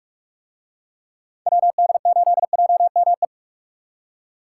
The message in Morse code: WD8JME

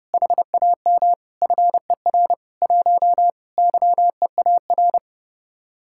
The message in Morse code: HAM FER 1 YEAR

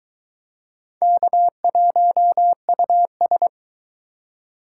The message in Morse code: K1US

